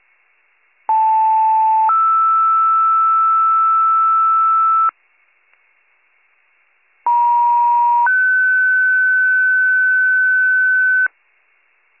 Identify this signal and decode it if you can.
Two-tone page, heard as a one-second tone, then a three-second tone, about 880 then 1320 Hz, about 950 then 1500 Hz. It is heard from 0.9 to 11.1 s.